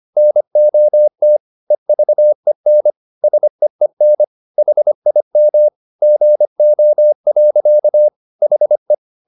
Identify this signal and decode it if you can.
Morse code, heard as a keyed tone, about 25 wpm, 600 Hz. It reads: NOT EVEN SEEN HIM GO. HE
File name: the old man and the sea part 1 25wpm